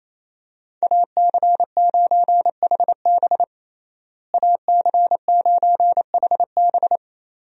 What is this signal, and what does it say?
Morse code, heard as a keyed tone, about 27 wpm, 700 Hz. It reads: AC9HB AC9HB
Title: USA Call Signs - 5 Characters - Repeated 3x - 28wpm